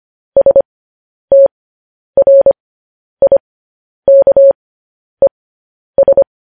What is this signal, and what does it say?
Morse code, heard as a keyed tone, about 25 wpm, spaced as for 10 wpm, 550 Hz. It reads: STRIKES